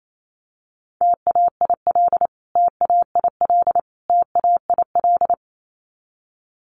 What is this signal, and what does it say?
Morse code, heard as a keyed tone, about 28 wpm, 700 Hz. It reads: TAIL TAIL TAIL
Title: Top 500 Words Repeated 28wpm